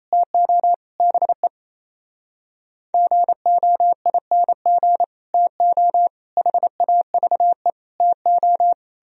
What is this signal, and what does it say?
Morse code, heard as a keyed tone, about 38 wpm, 700 Hz. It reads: TO BE GOING TO HAVE TO